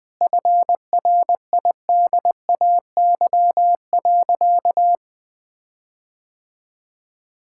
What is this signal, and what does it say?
Morse code, heard as a keyed tone, about 20 wpm, 700 Hz. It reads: FRIDAY.
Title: Sentences with 7-Word Phrases - Spoken Beginning - 20wpm